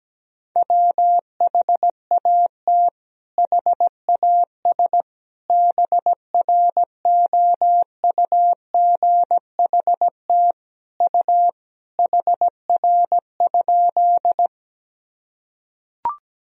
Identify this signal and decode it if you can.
Morse code, heard as a keyed tone, about 17 wpm, 700 Hz. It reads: WHAT HAS BROUGHT U HR?
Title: Sentences from Top 500 Words and Abbreviations 17wpm